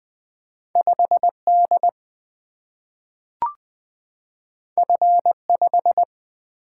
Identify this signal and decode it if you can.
Morse code, heard as a keyed tone, about 20 wpm, 700 Hz. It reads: 5 D F 5
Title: Single-Letter Single-Number 20wpm